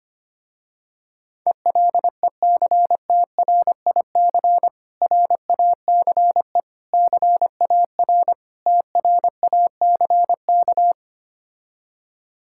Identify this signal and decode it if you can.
Morse code, heard as a keyed tone, about 25 wpm, 700 Hz. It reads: ELECTRIC RACE CAR TRACK
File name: practice / 25wpm / Difficult Words with Hints - 25wpm